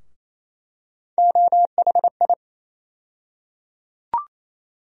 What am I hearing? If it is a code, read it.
Morse code, heard as a keyed tone, about 28 wpm, 700 Hz. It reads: OHI
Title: USA Call Sign Postfixes - 3 Letters 28wpm